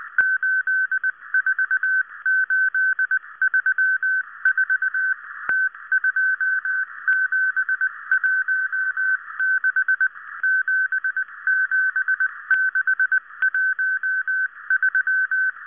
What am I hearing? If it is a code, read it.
Morse code, heard as a keyed tone, about 19 wpm, 1550 Hz. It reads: 84834 T2716 77613